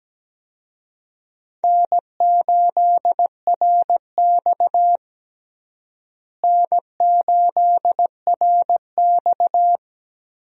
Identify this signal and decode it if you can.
Morse code, heard as a keyed tone, about 17 wpm, 700 Hz. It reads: N8RX N8RX
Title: USA Call Signs - 4 Characters - Repeated 3x - 17wpm